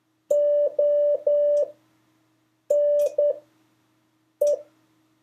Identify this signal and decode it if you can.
Morse code, heard as a keyed tone, about 10 wpm, spaced as for 6 wpm, 580 Hz. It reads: ONE
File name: 4of5-words-3